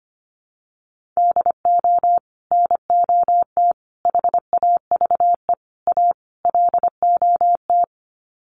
Morse code, heard as a keyed tone, about 25 wpm, 700 Hz. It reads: DO NOT HAVE A LOT
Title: Sets of 5 Words Encore 25wpm